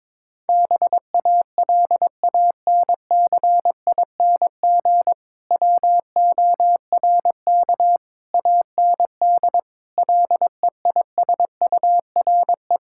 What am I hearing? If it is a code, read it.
Morse code, heard as a keyed tone, about 23 wpm, 700 Hz. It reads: BALANCING WORK AND LEISURE
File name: Difficult Sentences with Hints - 22wpm